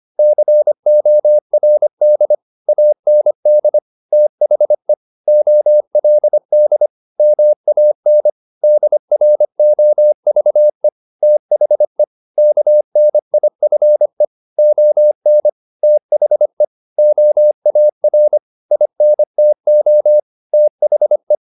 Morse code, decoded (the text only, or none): CORD AND THE OLD MAN DROVE THE KNIFE ON THE OAR INTO THE